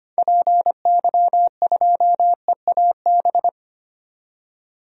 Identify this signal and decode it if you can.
Morse code, heard as a keyed tone, about 25 wpm, 700 Hz. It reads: PY2EAB
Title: DXCC-100 Call Signs - 6 Characters - 25wpm